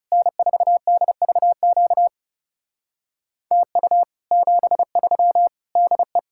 Morse code, heard as a keyed tone, about 37 wpm, 700 Hz. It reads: N4DVQ TU 73 DE
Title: Speed-Racing Common QSO Phrases - 30wpm